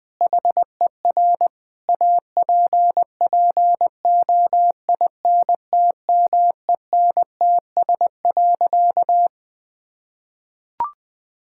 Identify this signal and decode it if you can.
Morse code, heard as a keyed tone, about 21 wpm, 700 Hz. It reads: HER APPOINTMENTS.
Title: Difficult Sentences with Hints - 20wpm